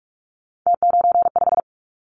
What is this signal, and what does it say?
Morse code, heard as a keyed tone, about 46 wpm, 690 Hz. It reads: T95
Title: Callsign Prefixes for Top 50 DXCC - 45wpm